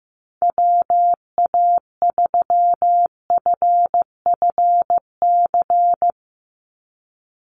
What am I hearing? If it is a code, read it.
Morse code, heard as a keyed tone, about 15 wpm, 700 Hz. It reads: WA3FFC
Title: USA Call Signs 15wpm